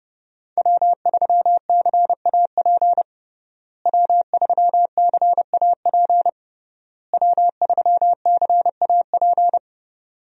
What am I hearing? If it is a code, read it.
Morse code, heard as a keyed tone, about 30 wpm, 700 Hz. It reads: W3CAP W3CAP W3CAP
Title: USA Call Signs - 5 Characters - Repeated 3x - 30wpm